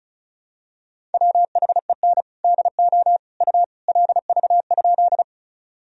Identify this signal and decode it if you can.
Morse code, heard as a keyed tone, about 34 wpm, 700 Hz. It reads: WHEN DO U LV?
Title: Sentences from Top 400 Words and Abbreviations 35wpm